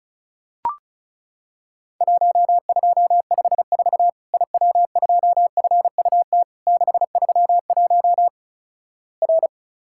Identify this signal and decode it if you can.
Morse code, heard as a keyed tone, about 36 wpm, 700 Hz. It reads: 1254 IW2FUT 631 R
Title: Worked All Europe DX Contest - QTC Exchange - 35wpm